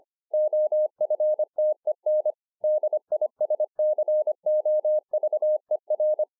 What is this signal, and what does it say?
Morse code, heard as a keyed tone, about 26 wpm, 610 Hz. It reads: OFTEN DISCOVER
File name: 2011-04-mp3 / BEGINNERS DIARY 25WPM